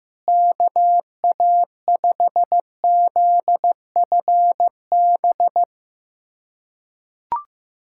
Morse code, heard as a keyed tone, about 15 wpm, 700 Hz. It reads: KA5ZFB
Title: USA Callsign DE Callsign - 6 Characters - 15wpm